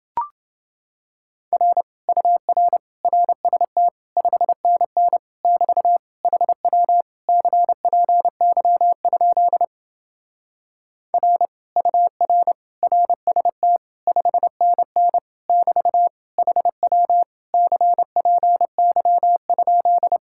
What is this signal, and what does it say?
Morse code, heard as a keyed tone, about 28 wpm, 700 Hz. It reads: R UR RST 5NN <BT> HW CPY? R UR RST 5NN <BT> HW CPY?